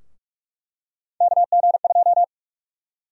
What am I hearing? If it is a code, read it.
Morse code, heard as a keyed tone, about 44 wpm, 700 Hz. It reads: KG2
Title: USA Call Sign Prefixes - Repeated 2x - 45wpm